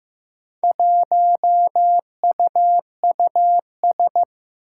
Morse code, heard as a keyed tone, about 15 wpm, 700 Hz. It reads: 1UUS
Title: Four Challenging Letter-Number Combinations 15wpm